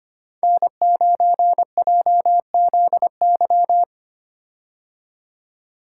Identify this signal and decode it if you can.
Morse code, heard as a keyed tone, about 25 wpm, 700 Hz. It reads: N9JZY
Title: K1USN SST QSO Practice - 25wpm